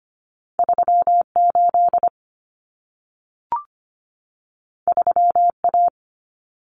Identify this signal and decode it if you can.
Morse code, heard as a keyed tone, about 25 wpm, 700 Hz. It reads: 38 3A
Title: Common Cut Numbers - 2-Digit 25wpm